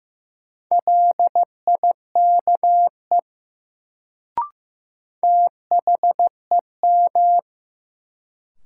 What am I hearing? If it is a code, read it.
Morse code, heard as a keyed tone, about 15 wpm, 700 Hz. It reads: LIKE THEM
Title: Sentences from Top 100 Words - New Word Drill - 15wpm